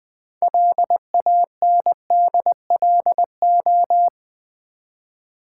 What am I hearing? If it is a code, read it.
Morse code, heard as a keyed tone, about 20 wpm, 700 Hz. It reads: LANDLO
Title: Build A Word - 3 Letters At A Time - 8-Letter Words - 20wpm